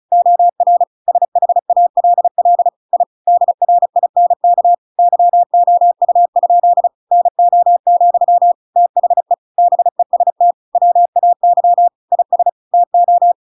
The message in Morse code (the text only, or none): OR SHALL I DRINK YOU? NO, THE BEST WAY IS TO